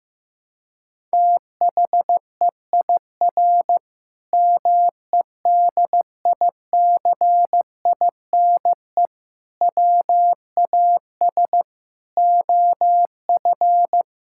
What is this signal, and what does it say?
Morse code, heard as a keyed tone, about 15 wpm, 700 Hz. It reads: THEIR MEDICINE WAS OF
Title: Sentences from Top 1000 Words - Review Part 6 - 15wpm